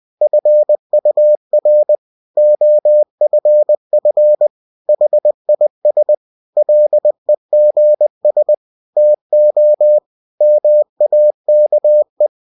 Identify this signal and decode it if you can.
Morse code, heard as a keyed tone, about 20 wpm, 600 Hz. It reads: FUR OFF HIS LEGS TO MAKE
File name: cw-books / winnie-the-pooh / winnie-20wpm-page-0094